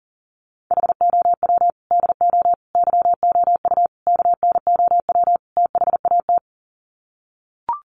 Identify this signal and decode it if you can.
Morse code, heard as a keyed tone, about 40 wpm, 700 Hz. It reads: HOW DO YOU KNOW THAT
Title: Sets of 5 Words Encore 40wpm